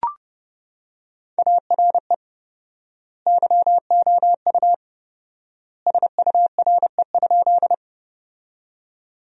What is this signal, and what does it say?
Morse code, heard as a keyed tone, about 30 wpm, 700 Hz. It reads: ARE YOU SURE?
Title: 4x Word-Spacing - Sentences from Top 300 Words - 30wpm